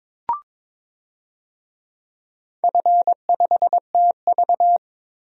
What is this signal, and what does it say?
Morse code, heard as a keyed tone, about 22 wpm, 690 Hz. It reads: F5TV
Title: DXCC-100 Call Signs - 4 Characters - 22wpm